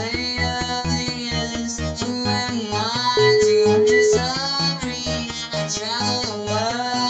Ringback tone (Europe, 425 Hz), with noise.